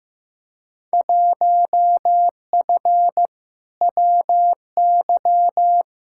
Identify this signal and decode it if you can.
Morse code, heard as a keyed tone, about 15 wpm, 700 Hz. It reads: 1F WY
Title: Field Day Exchanges Encore - 15wpm